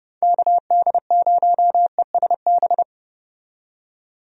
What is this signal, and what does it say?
Morse code, heard as a keyed tone, about 30 wpm, 700 Hz. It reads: KD0ESB
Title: USA Call Signs - 6 Characters - 30wpm